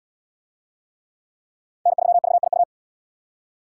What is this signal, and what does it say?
Morse code, heard as a keyed tone, about 77 wpm, 700 Hz. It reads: N3VIU